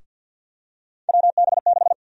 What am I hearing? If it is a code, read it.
Morse code, heard as a keyed tone, about 48 wpm, 700 Hz. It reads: WB6
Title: USA Call Sign Prefixes - Repeated 2x - 50wpm